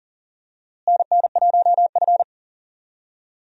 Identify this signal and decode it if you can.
Morse code, heard as a keyed tone, about 37 wpm, 700 Hz. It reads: NN1F
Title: USA Call Signs - 4 Characters - Repeated 2x - 40wpm